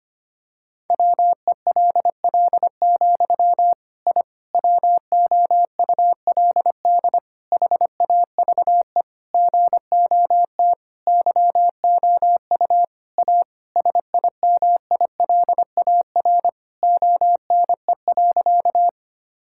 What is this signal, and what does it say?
Morse code, heard as a keyed tone, about 25 wpm, 700 Hz. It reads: WELL, I WOULD HAVE GOT YOU A SIMILAR ONE.
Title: Sentences from Top 1000 Words - Review Part 5 - 25wpm